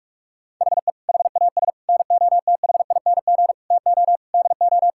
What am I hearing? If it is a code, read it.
Morse code, heard as a keyed tone, about 46 wpm, 700 Hz. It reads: HE HAS NOTHING TO DO
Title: Sentences from Top 1000 Words - Review Part 5 - 45wpm